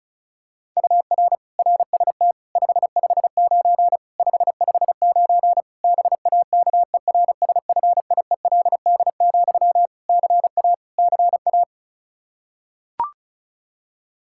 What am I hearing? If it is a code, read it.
Morse code, heard as a keyed tone, about 36 wpm, 700 Hz. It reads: UR RST 559 559 BAKERSFIELD, CA CA
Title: K1USN SST QSO Practice 35wpm